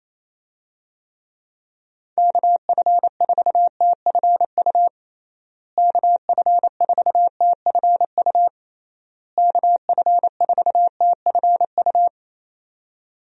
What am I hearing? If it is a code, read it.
Morse code, heard as a keyed tone, about 28 wpm, 700 Hz. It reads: KF4TFU KF4TFU KF4TFU
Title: USA Call Signs - 6 Characters - Repeated 3x - 28wpm